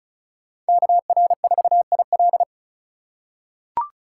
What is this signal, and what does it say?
Morse code, heard as a keyed tone, about 36 wpm, 700 Hz. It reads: KR4IL
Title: USA Call Signs - 5 Characters - 2x Character Spacing - 35wpm